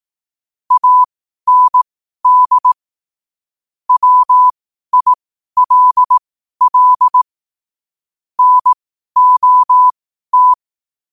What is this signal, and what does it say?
Morse code, heard as a keyed tone, about 18 wpm, spaced as for 12 wpm, 1000 Hz. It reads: AND WILL NOT
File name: qotdfiles / 12wpm